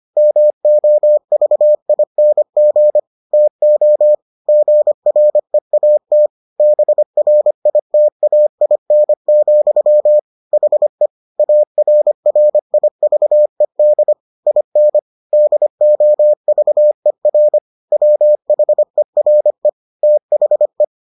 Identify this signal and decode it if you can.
Morse code, heard as a keyed tone, about 25 wpm, 600 Hz. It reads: MOVING TO GREAT BRITAIN, HE ARRIVED IN DOVER WHERE THE